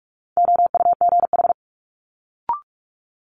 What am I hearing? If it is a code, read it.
Morse code, heard as a keyed tone, about 46 wpm, 700 Hz. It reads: OUGH
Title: Multi-letter Phonemes Quick Fire 45wpm